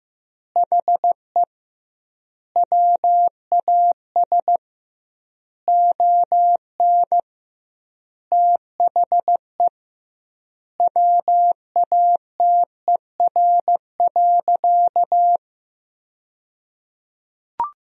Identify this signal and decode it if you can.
Morse code, heard as a keyed tone, about 15 wpm, 700 Hz. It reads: HE WAS ON THE WATER.
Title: 2x Word-Spacing - Sentences from Top 500 Words - 15wpm